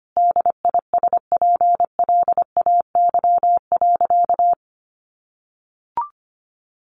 Morse code, heard as a keyed tone, about 25 wpm, 700 Hz. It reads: DISPLAY.
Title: Sentences from Top 1200 Words 25wpm